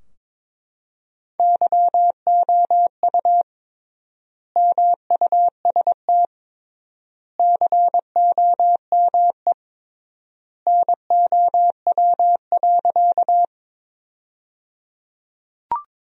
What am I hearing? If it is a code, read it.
Morse code, heard as a keyed tone, about 22 wpm, 700 Hz. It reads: YOU MUST COME NOW.
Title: 3x Word-Spacing - Sentences from Top 200 Words - 22wpm